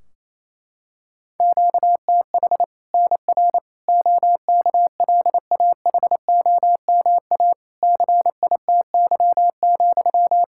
Morse code, heard as a keyed tone, about 28 wpm, 700 Hz. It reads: QTH NR OKLAHOMA CITY,
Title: Practice QSO - Exchange 1 - Beginning - 28wpm